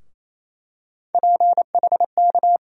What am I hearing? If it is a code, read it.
Morse code, heard as a keyed tone, about 28 wpm, 700 Hz. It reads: PHK
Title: USA Call Sign Postfixes - 3 Letters 28wpm